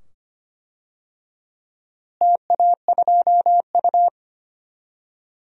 Morse code, heard as a keyed tone, about 25 wpm, 700 Hz. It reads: TA2U
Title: DXCC-100 Call Signs - 4 Characters - Repeated 2x - 25wpm